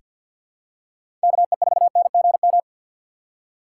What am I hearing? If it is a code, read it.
Morse code, heard as a keyed tone, about 47 wpm, 700 Hz. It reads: KE4NGM
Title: USA Call Signs - 3x Character Spacing - 50wpm